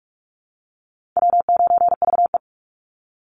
Morse code, heard as a keyed tone, about 44 wpm, 700 Hz. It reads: W9VE
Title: Easy Call Signs Repeated 3x - 45wpm